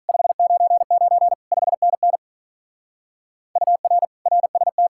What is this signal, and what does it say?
Morse code, heard as a keyed tone, about 45 wpm, 700 Hz. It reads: 599 5NN UR RST